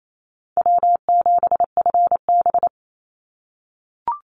Morse code, heard as a keyed tone, about 28 wpm, 700 Hz. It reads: W7FB
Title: Call Signs - 28wpm